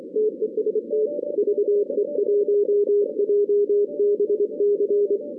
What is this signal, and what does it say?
Morse code, heard as a keyed tone, about 24 wpm, 440 Hz. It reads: TEST VE1JBC
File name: 2016 arrl dx cw / 2016-ARRL-DX-CW-K5TR- - 0-41